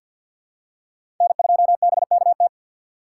Morse code, heard as a keyed tone, about 51 wpm, 700 Hz. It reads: N2BKT
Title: USA Call Signs - 5 Characters - 6x Character Spacing - 50wpm